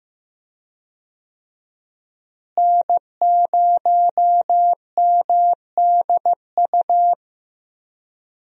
Morse code, heard as a keyed tone, about 15 wpm, 700 Hz. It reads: N0MDU
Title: USA Call Signs - 5 Characters - Repeated 3x - 15wpm